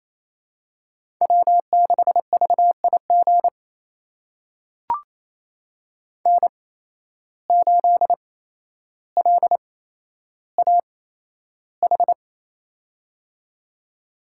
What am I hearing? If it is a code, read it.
Morse code, heard as a keyed tone, about 28 wpm, spaced as for 12 wpm, 700 Hz. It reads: W6VIG N8LAH
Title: USA Call Signs - 5 Characters - 8x Character Spacing - 28wpm